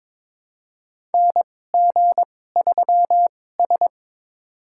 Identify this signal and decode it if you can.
Morse code, heard as a keyed tone, about 22 wpm, spaced as for 16 wpm, 700 Hz. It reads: NG3S
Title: USA Call Signs - 4 Characters - 2x Character Spacing - 22wpm